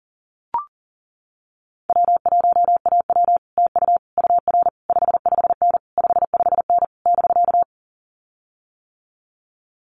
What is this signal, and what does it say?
Morse code, heard as a keyed tone, about 40 wpm, 700 Hz. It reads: W1AW TU UR 55N 55N <BK>
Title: Common QSO Phrases Encore 40wpm